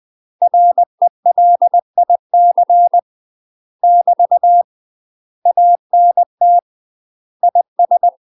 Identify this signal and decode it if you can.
Morse code, heard as a keyed tone, about 20 wpm, 700 Hz. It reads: RELIC <BT> ANT IS
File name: UK MASTER QSOs TRAINING / 20WPM / UK MASTER QSOs 0051